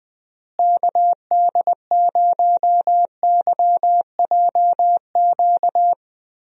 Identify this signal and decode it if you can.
Morse code, heard as a keyed tone, about 20 wpm, 700 Hz. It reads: KD0YJQ
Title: USA Call Signs - 6 Characters - Repeated 3x - 20wpm